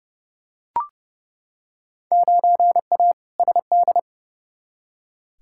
Morse code, heard as a keyed tone, about 29 wpm, 700 Hz. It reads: 9A SD
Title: Field Day Exchanges Encore - 30wpm